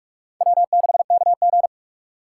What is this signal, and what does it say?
Morse code, heard as a keyed tone, about 51 wpm, 700 Hz. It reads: W6KG
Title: Speed Racing - USA 1x2 Call Signs 45wpm